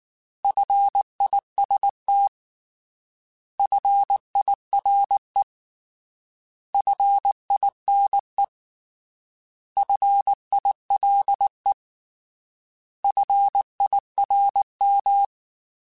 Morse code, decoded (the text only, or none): FIST FIRE FINE FILE FIRM